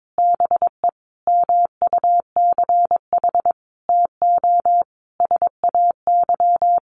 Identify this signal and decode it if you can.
Morse code, heard as a keyed tone, about 22 wpm, 700 Hz. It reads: BE MUCH TO SAY
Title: Sentences from Top 1200 Words 22wpm